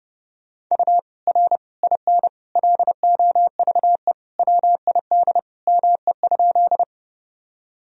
Morse code, heard as a keyed tone, about 30 wpm, 700 Hz. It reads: U R IN LOVE WID ME?
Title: Sentences from Top 400 Words and Abbreviations 30wpm